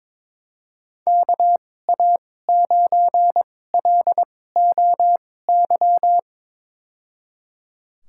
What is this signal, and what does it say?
Morse code, heard as a keyed tone, about 21 wpm, spaced as for 16 wpm, 700 Hz. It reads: KA9LOY
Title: USA Call Signs - 2x Character Spacing - 22wpm